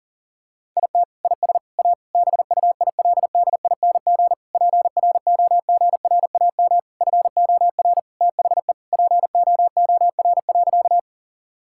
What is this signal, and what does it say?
Morse code, heard as a keyed tone, about 40 wpm, 700 Hz. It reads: IT IS A BUILDING PROGRAM FOR THE POOR.
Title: Sentences from Top 1000 Words - Review Part 5 - 40wpm